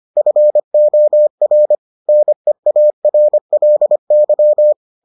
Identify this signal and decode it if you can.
Morse code, heard as a keyed tone, about 25 wpm, 600 Hz. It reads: FOR NEARLY